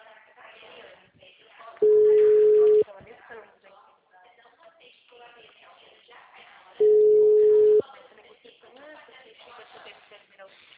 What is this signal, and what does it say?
Ringback tone (Europe, 425 Hz), with speech.
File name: out- -2006- - - .57594